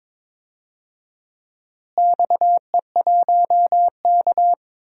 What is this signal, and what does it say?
Morse code, heard as a keyed tone, about 22 wpm, 700 Hz. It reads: XE1K
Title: DXCC-100 Call Signs - 4 Characters - Repeated 2x - 22wpm